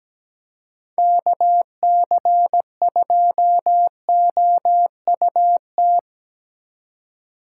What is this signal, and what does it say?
Morse code, heard as a keyed tone, about 17 wpm, 700 Hz. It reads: KC2OUT
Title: USA Call Signs - 6 Characters - 3x Character Spacing - 17wpm